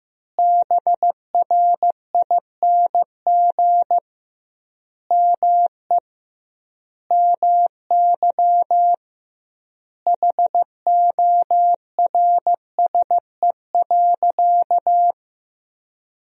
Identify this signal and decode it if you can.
Morse code, heard as a keyed tone, about 15 wpm, 700 Hz. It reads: BRING ME MY HORSE.
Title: 2x Word-Spacing - Sentences from Top 500 Words - 15wpm